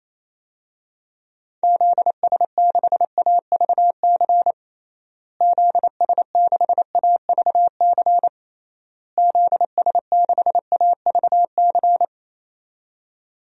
Morse code, heard as a keyed tone, about 28 wpm, 700 Hz. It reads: ZS6AVC ZS6AVC ZS6AVC
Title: DXCC-100 Call Signs - 6 Characters - Repeated 3x - 28wpm